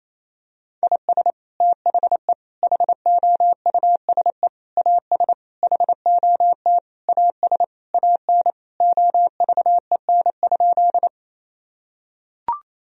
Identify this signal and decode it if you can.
Morse code, heard as a keyed tone, about 28 wpm, 700 Hz. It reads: IS THE HOUSE AS HOT AS AN OVEN?
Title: Sentences from Top 1000 Words - Review Part 4 - 28wpm